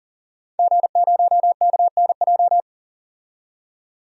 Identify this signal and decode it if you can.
Morse code, heard as a keyed tone, about 38 wpm, 700 Hz. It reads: G0KNJ
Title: DXCC-100 Call Signs - 5 Characters - 40wpm